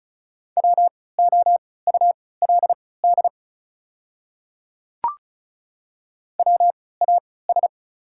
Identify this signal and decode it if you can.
Morse code, heard as a keyed tone, about 36 wpm, spaced as for 20 wpm, 690 Hz. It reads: WOULD WAS
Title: 3x Character Spacing - Top 100 Words - 35wpm